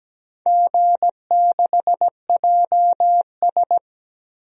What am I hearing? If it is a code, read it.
Morse code, heard as a keyed tone, about 17 wpm, 700 Hz. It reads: G6JS
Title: DXCC-100 Call Signs - 4 Characters - 17wpm